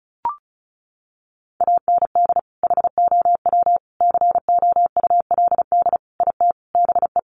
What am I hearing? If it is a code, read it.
Morse code, heard as a keyed tone, about 34 wpm, 700 Hz. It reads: AND HOW COULD IT BE
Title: Sentences from Top 1200 Words 35wpm